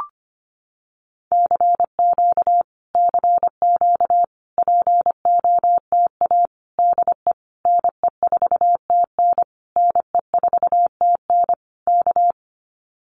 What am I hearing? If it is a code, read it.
Morse code, heard as a keyed tone, about 25 wpm, 700 Hz. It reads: CQ CQ POTA DE NE4TN NE4TN K
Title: Common QSO Phrases Encore 25wpm